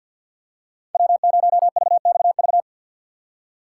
Morse code, heard as a keyed tone, about 53 wpm, 700 Hz. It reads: W0VXV
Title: USA Call Signs - 5 Characters - 4x Character Spacing - 50wpm